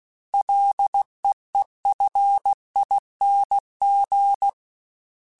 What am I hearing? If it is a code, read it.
Morse code, heard as a keyed tone, about 16 wpm, 800 Hz. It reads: LEEFING
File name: Ls / Single L m3s / Single L Prefixe 3x2 Twice @15